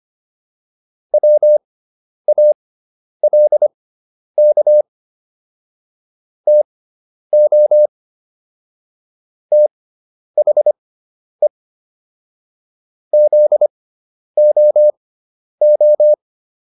Morse code, heard as a keyed tone, about 25 wpm, spaced as for 10 wpm, 600 Hz. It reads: WALK TO THE ZOO